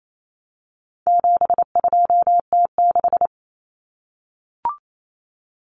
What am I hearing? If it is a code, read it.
Morse code, heard as a keyed tone, about 28 wpm, 700 Hz. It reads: 72T6
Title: Common Cut Numbers - 4-Digit 28wpm